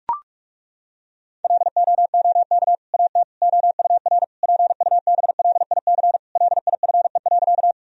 Morse code, heard as a keyed tone, about 41 wpm, 700 Hz. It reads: LOOK AT OUR PUBLIC LIFE.
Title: Sentences from Top 1000 Words - Review Part 3 - 45wpm